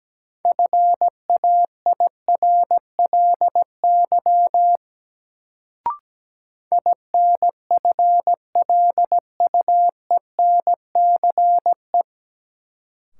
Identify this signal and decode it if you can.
Morse code, heard as a keyed tone, about 17 wpm, 700 Hz. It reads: FAIRLY INFLUENCE
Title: Sentences from Top 1100 Words - New Word Drill Alt - 17wpm